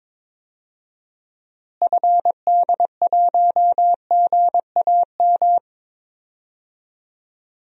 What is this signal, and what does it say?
Morse code, heard as a keyed tone, about 21 wpm, 700 Hz. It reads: FD1GAM
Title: DXCC-100 Call Signs - 6 Characters - Repeated 2x - 22wpm